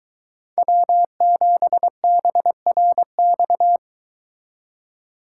Morse code, heard as a keyed tone, about 23 wpm, 700 Hz. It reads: W7BRX